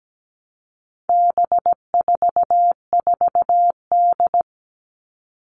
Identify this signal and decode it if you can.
Morse code, heard as a keyed tone, about 17 wpm, 700 Hz. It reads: B44D
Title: All Cut Numbers - 4-Digit 17wpm